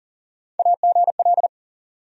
Morse code, heard as a keyed tone, about 40 wpm, 700 Hz. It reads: AGL
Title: USA Call Sign Postfixes - 3 Letters 40wpm